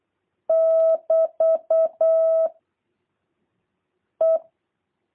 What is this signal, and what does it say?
Morse code, heard as a keyed tone, about 8 wpm, 650 Hz. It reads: <BT> E
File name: jh8cbh / 2023cw